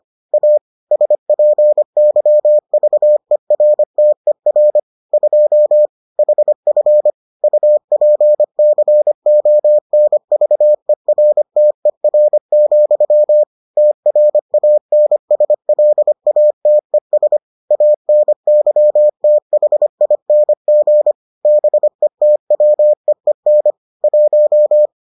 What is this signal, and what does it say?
Morse code, heard as a keyed tone, about 25 wpm, 600 Hz. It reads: A SPYVERTER 2 HF UPCONVERTER, TRANSLATES ANYTHING BETWEEN 1